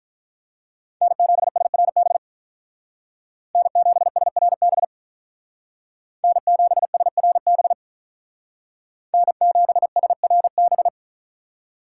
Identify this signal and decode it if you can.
Morse code, heard as a keyed tone, about 45 wpm, 700 Hz. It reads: N7SRB N7SRB N7SRB N7SRB